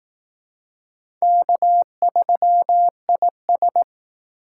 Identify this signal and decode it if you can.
Morse code, heard as a keyed tone, about 18 wpm, 700 Hz. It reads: K3IS
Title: Speed Racing - USA 1x2 Call Signs 15wpm